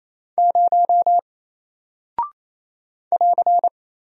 Morse code, heard as a keyed tone, about 28 wpm, 700 Hz. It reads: 0 <AR>
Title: Single Character (Letter:Number:Punctuation:Prosign) 28wpm